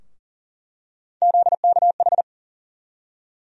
Morse code, heard as a keyed tone, about 40 wpm, 700 Hz. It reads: ZKH
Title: USA Call Sign Postfixes - 3 Letters 40wpm